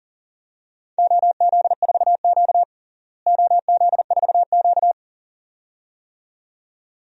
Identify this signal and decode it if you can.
Morse code, heard as a keyed tone, about 40 wpm, 700 Hz. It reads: OZ4Q OZ4Q
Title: DXCC-100 Call Signs - 4 Characters - Repeated 2x - 40wpm